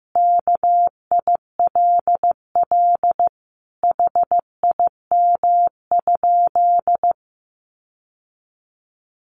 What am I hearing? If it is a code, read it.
Morse code, heard as a keyed tone, about 15 wpm, 700 Hz. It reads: KILL HIM?
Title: Sentences from Top 1400 Words 15wpm